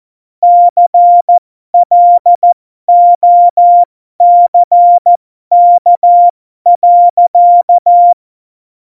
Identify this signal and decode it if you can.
Morse code, heard as a keyed tone, about 14 wpm, spaced as for 12 wpm, 700 Hz. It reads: CLOCK.
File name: Morsekurs / Witze